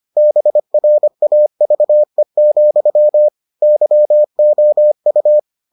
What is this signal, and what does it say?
Morse code, heard as a keyed tone, about 25 wpm, 600 Hz. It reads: BRAVE, YOU